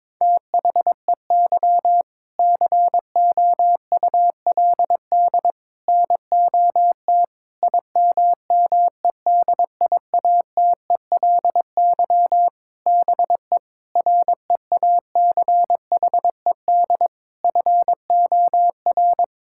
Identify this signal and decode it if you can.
Morse code, heard as a keyed tone, about 22 wpm, 700 Hz. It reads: THEY COULD NOT IMMEDIATELY BE REACHED FOR
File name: Sentences with 7-Word Phrases - 22wpm